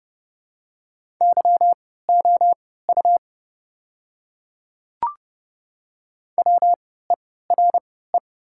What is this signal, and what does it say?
Morse code, heard as a keyed tone, about 30 wpm, spaced as for 17 wpm, 690 Hz. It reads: YOU WERE
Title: 3x Character Spacing - Top 200 Words - 30wpm